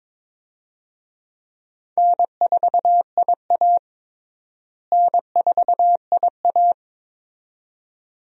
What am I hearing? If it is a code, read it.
Morse code, heard as a keyed tone, about 22 wpm, 700 Hz. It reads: N4IA N4IA
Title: USA Call Signs - 4 Characters - Repeated 2x - 22wpm